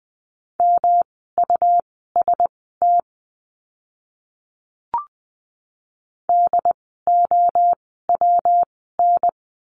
Morse code, heard as a keyed tone, about 20 wpm, spaced as for 15 wpm, 700 Hz. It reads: MUST DOWN
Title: 115 Lesson 19f - Letter w - Words 20wpm